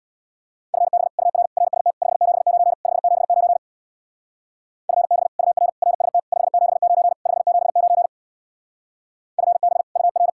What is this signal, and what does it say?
Morse code, heard as a keyed tone, about 64 wpm, 700 Hz. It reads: FB UR RST 579 579 FB UR RST 579 579 FB UR